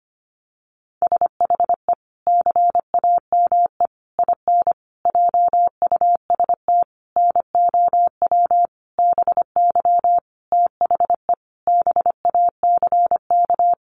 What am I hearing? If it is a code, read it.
Morse code, heard as a keyed tone, about 25 wpm, 700 Hz. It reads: SHE CAME IN JUST NOW BY THE BACK
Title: Sentences from Top 300 Words 25wpm